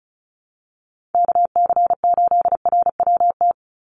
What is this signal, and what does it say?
Morse code, heard as a keyed tone, about 34 wpm, 700 Hz. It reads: KC8RWT
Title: USA Call Signs 35wpm